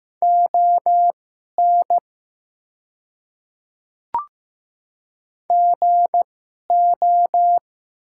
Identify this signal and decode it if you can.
Morse code, heard as a keyed tone, about 15 wpm, spaced as for 11 wpm, 700 Hz. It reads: ON GO
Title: 2x Character Spacing - Top 300 Words - 15wpm